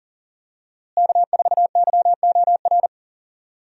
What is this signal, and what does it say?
Morse code, heard as a keyed tone, about 40 wpm, 700 Hz. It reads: K4YOR
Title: USA Call Signs - 5 Characters - Repeated 2x - 40wpm